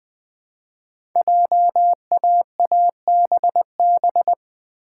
Morse code, heard as a keyed tone, about 20 wpm, 700 Hz. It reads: JAABB
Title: Code Groups - 5 Letters - 20wpm